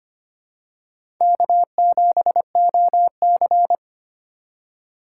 Morse code, heard as a keyed tone, about 25 wpm, 700 Hz. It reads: K7OC
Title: USA Call Signs - 4 Characters - 3x Character Spacing - 25wpm